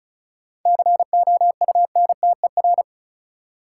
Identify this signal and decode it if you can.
Morse code, heard as a keyed tone, about 36 wpm, 700 Hz. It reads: COUNTER
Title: Seven Letter Words - 35wpm